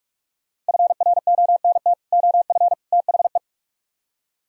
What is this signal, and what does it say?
Morse code, heard as a keyed tone, about 56 wpm, 700 Hz. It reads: FRONT OF THE
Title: Speed-Racing - Sets of 3 Words 45wpm